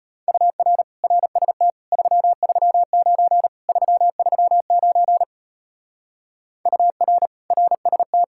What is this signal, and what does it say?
Morse code, heard as a keyed tone, about 31 wpm, 700 Hz. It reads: UR RST 339 339 UR RST
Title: Speed-Racing Common QSO Phrases Encore - 25wpm